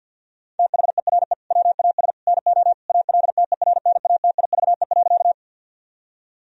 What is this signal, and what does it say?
Morse code, heard as a keyed tone, about 50 wpm, 700 Hz. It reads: THERE WAS NO ALTERNATIVE.
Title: Sentences from Top 1000 Words - Review Part 4 - 50wpm